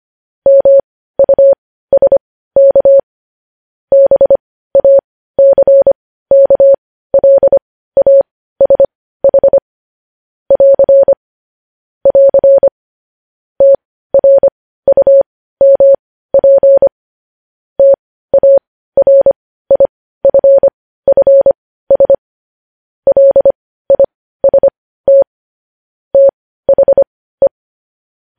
Morse code, heard as a keyed tone, about 25 wpm, spaced as for 15 wpm, 550 Hz. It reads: MUSK BACKLASH <AR> <AR> TRUMP TARIFFS LIST THE
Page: news headlines in morse code at 15 wpm